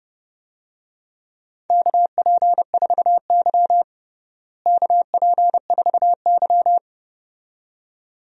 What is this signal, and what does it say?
Morse code, heard as a keyed tone, about 30 wpm, 700 Hz. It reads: KP4Y KP4Y
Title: DXCC-100 Call Signs - 4 Characters - Repeated 2x - 30wpm